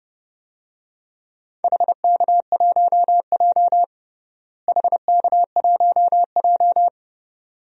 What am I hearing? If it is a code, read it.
Morse code, heard as a keyed tone, about 32 wpm, 700 Hz. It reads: HK1J HK1J
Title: DXCC-100 Call Signs - 4 Characters - Repeated 2x - 30wpm